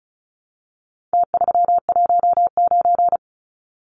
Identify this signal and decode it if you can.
Morse code, heard as a keyed tone, about 34 wpm, 700 Hz. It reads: T319
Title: Numbers - Contest Style - 4-Digit 35wpm